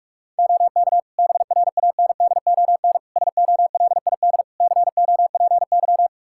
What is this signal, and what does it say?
Morse code, heard as a keyed tone, about 46 wpm, 700 Hz. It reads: OK BRANDON SOLID COPY
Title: K1USN SST QSO Practice 45wpm